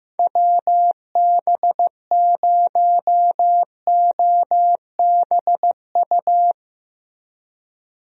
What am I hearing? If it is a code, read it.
Morse code, heard as a keyed tone, about 15 wpm, 700 Hz. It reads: WB0OBU
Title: USA Call Signs - 6 Characters - Repeated 3x - 15wpm